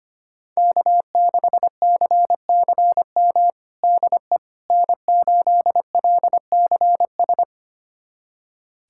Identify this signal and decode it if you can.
Morse code, heard as a keyed tone, about 25 wpm, 700 Hz. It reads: K6CCM DE N8LCS
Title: USA Callsign DE Callsign - 5 Characters - 25wpm